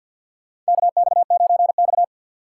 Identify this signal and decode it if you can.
Morse code, heard as a keyed tone, about 52 wpm, 700 Hz. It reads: KX9X
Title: USA Call Signs - 4x Character Spacing - 50wpm